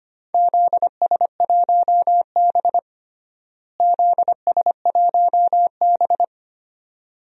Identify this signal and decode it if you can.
Morse code, heard as a keyed tone, about 25 wpm, 700 Hz. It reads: ZS1B ZS1B
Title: DXCC-100 Call Signs - 4 Characters - Repeated 2x - 25wpm